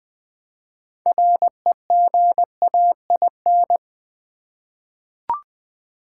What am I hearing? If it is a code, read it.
Morse code, heard as a keyed tone, about 21 wpm, 700 Hz. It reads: REGAIN
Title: 6 Letter Words - First 3 Letters Given - 20wpm